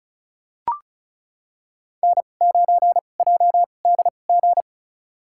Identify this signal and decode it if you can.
Morse code, heard as a keyed tone, about 36 wpm, spaced as for 25 wpm, 700 Hz. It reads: N9JDG
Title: USA Call Signs - 5 Characters - 2x Character Spacing - 35wpm